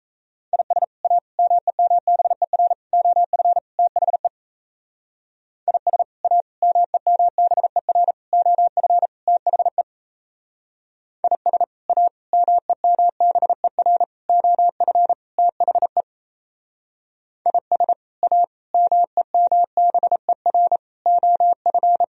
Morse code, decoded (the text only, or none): IS A MEMBER OF THE IS A MEMBER OF THE IS A MEMBER OF THE IS A MEMBER OF